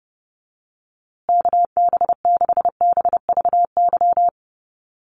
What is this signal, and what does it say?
Morse code, heard as a keyed tone, about 30 wpm, 700 Hz. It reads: KB6BVY
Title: USA Call Signs Repeated 2x 30wpm